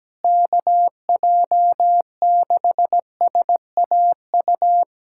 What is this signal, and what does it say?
Morse code, heard as a keyed tone, about 17 wpm, 700 Hz. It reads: KJ6SAU
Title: USA Callsign DE Callsign 17wpm